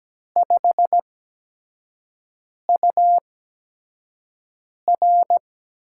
Morse code, heard as a keyed tone, about 17 wpm, spaced as for 5 wpm, 700 Hz. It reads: 5UR
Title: USA Call Signs - 4 Characters - 8x Character Spacing - 17wpm